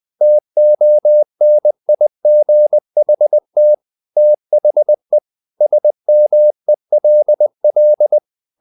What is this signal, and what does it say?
Morse code, heard as a keyed tone, about 20 wpm, 600 Hz. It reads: TONIGHT THE SMELL
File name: the old man and the sea / the old man and the sea part 1 20wpm